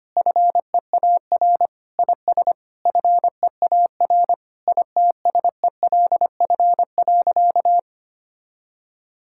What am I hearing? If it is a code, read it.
Morse code, heard as a keyed tone, about 25 wpm, 700 Hz. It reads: FEAR IS FEAR ITSELF.
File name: practice / 25wpm / Sentences with 9-Word Phrases - 25wpm